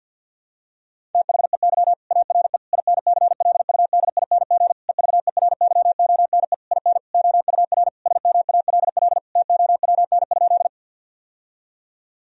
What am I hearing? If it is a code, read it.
Morse code, heard as a keyed tone, about 52 wpm, 700 Hz. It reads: THEY ARE INCLUDING EVERYONE IN OUR SMALL TOWN?
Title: Sentences from Top 1000 Words - Review Part 4 - 50wpm